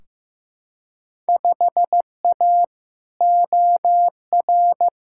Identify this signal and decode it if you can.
Morse code, heard as a keyed tone, about 15 wpm, 700 Hz. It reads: 5A OR
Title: Field Day Exchanges - 15wpm